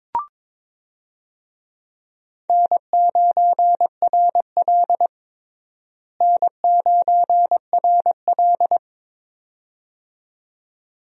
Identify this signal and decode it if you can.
Morse code, heard as a keyed tone, about 22 wpm, 700 Hz. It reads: N9RL N9RL
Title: USA Call Signs - 4 Characters - Repeated 2x - 22wpm